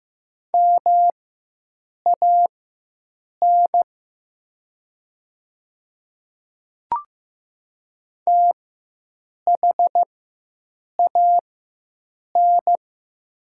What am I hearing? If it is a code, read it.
Morse code, heard as a keyed tone, about 15 wpm, spaced as for 7 wpm, 700 Hz. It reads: MAN THAN
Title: 4x Character Spacing - Top 200 Words - 15wpm